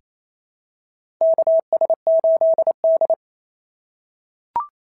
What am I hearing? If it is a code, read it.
Morse code, heard as a keyed tone, about 28 wpm, 690 Hz. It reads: KS8D
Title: SOTA QSO Practice 28wpm